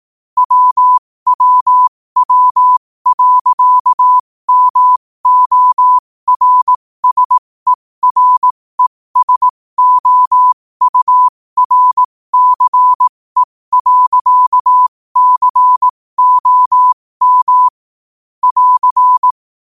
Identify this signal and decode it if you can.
Morse code, heard as a keyed tone, about 18 wpm, spaced as for 15 wpm, 1000 Hz. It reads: WWW.MORSERESOURCE.COM <AR>
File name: qotdfiles / 15wpm